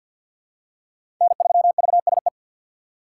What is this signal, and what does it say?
Morse code, heard as a keyed tone, about 47 wpm, 700 Hz. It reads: N3VSE
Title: USA Call Signs - 5 Characters - 2x Character Spacing - 50wpm